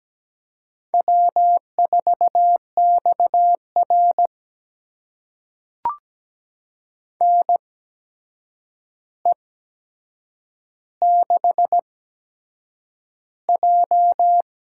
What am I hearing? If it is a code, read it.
Morse code, heard as a keyed tone, about 17 wpm, spaced as for 7 wpm, 700 Hz. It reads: W4XR NE6J
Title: USA Call Signs - 4 Characters - 8x Character Spacing - 17wpm